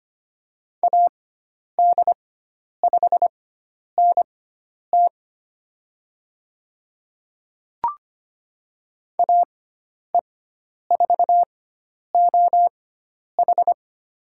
Morse code, heard as a keyed tone, about 25 wpm, spaced as for 10 wpm, 700 Hz. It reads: AD5NT AE4OH
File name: Lesson 16g - Letter c - Call Signs 25wpm